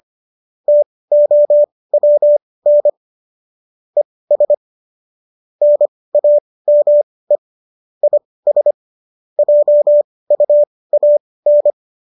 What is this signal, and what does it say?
Morse code, heard as a keyed tone, about 25 wpm, spaced as for 18 wpm, 590 Hz. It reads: TOWN ES NAME IS JUAN
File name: qso204 18